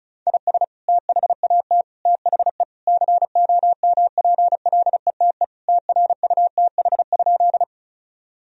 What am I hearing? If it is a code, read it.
Morse code, heard as a keyed tone, about 36 wpm, 700 Hz. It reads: IS THAT THE COMPLETE TRUTH?
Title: Sentences from Top 1000 Words - Review Part 2 - 35wpm